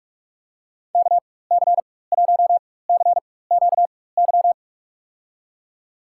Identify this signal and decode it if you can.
Morse code, heard as a keyed tone, about 46 wpm, spaced as for 21 wpm, 700 Hz. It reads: KC1CQY
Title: USA Call Signs - 6 Characters - 4x Character Spacing - 45wpm